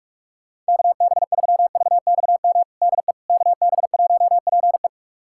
Morse code, heard as a keyed tone, about 46 wpm, 700 Hz. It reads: KB3VXM DE KB1PE
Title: USA Callsign DE Callsign 45wpm